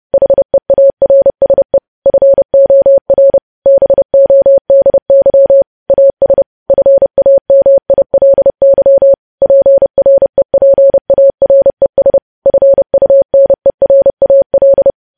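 Morse code, heard as a keyed tone, about 30 wpm, 550 Hz. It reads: HEARSE FOR BODY AS FAMILY PREPARES FUNERAL
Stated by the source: Morse code sending of news headlines at 30 wpm.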